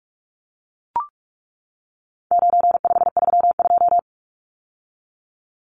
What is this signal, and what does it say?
Morse code, heard as a keyed tone, about 44 wpm, 700 Hz. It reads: 9532
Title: Numbers - 4-Digit 45wpm